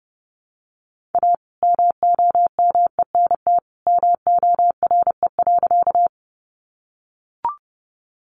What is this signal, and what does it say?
Morse code, heard as a keyed tone, about 30 wpm, 700 Hz. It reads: A MOMENT MORE.
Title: Sentences from Top 1000 Words 30wpm